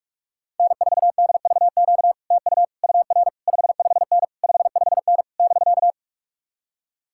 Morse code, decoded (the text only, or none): N4DVQ TU UR 55N 55N <BK>